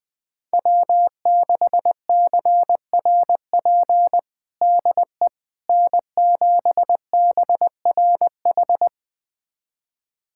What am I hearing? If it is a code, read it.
Morse code, heard as a keyed tone, about 20 wpm, 700 Hz. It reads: W6CRP DE N7BRH
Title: USA Callsign DE Callsign - 5 Characters - 20wpm